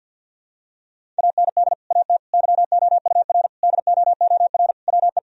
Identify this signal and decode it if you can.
Morse code, heard as a keyed tone, about 52 wpm, 700 Hz. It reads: AND AT YOUR DOOR WE
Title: Sentences from Top 1000 Words - Review Part 1 - 50wpm